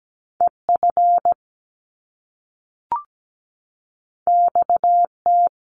Morse code, heard as a keyed tone, about 17 wpm, 700 Hz. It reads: EF XT
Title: Top Two Letter Combinations in Words 17wpm